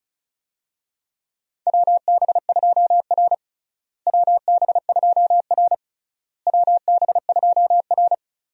Morse code, heard as a keyed tone, about 36 wpm, 700 Hz. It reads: WB2R WB2R WB2R
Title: USA Call Signs - 4 Characters - Repeated 3x - 35wpm